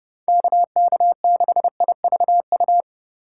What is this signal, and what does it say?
Morse code, heard as a keyed tone, about 30 wpm, 700 Hz. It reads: KK6IVU